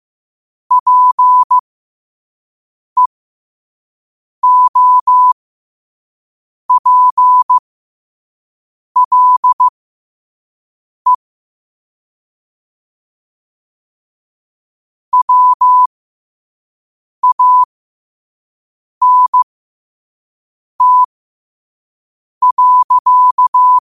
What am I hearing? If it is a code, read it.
Morse code, heard as a keyed tone, about 15 wpm, spaced as for 5 wpm, 1000 Hz. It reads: PEOPLE WANT.
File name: qotdfiles / 5wpm